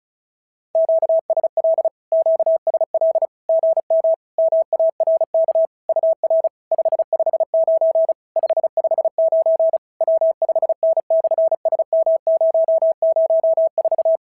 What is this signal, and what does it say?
Morse code, heard as a keyed tone, about 34 wpm, 650 Hz. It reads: QSL QSL GM MARK UR 559 559 W5N/SM004
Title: SOTA QSO Practice 35wpm